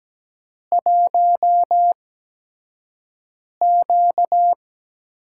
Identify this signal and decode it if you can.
Morse code, heard as a keyed tone, about 17 wpm, spaced as for 5 wpm, 700 Hz. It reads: 1Q
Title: USA Call Signs - 5 Characters - 8x Character Spacing - 17wpm